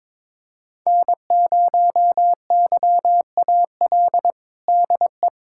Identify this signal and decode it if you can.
Morse code, heard as a keyed tone, about 22 wpm, 700 Hz. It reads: N0YAL DE
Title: USA Callsign DE Callsign - 5 Characters - 22wpm